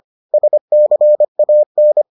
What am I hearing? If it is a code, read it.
Morse code, heard as a keyed tone, about 25 wpm, 600 Hz. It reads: SCAN